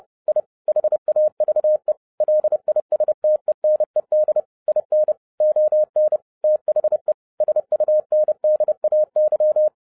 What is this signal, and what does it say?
Morse code, heard as a keyed tone, about 30 wpm, 610 Hz. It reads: I HAVE LISTENED IN ON THE SUNDAY